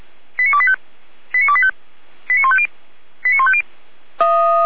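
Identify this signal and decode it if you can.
Five-tone selective call (ZVEI), heard as quick runs of five tones, tones of about 70 ms. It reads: 89285 89285 89150 89150
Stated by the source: ZVEI five-tone selective call received on an SDR.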